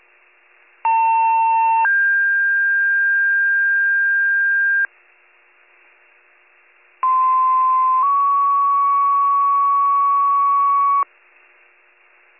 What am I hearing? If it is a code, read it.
Two-tone page, heard as a one-second tone, then a three-second tone, about 900 then 1600 Hz, about 1040 then 1120 Hz. It is heard from 0.8 to 11.0 s.